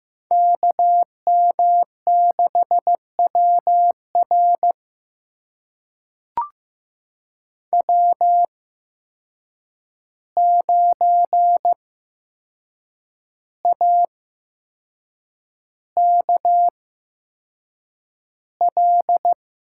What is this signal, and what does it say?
Morse code, heard as a keyed tone, about 15 wpm, spaced as for 6 wpm, 700 Hz. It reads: KM6WR W9AKL
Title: USA Call Signs - 5 Characters - 8x Character Spacing - 15wpm